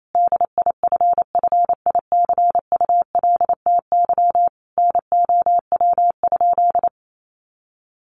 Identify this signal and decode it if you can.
Morse code, heard as a keyed tone, about 28 wpm, 700 Hz. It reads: DIFFICULTY NOW?
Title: Sentences from Top 1500 Words 28wpm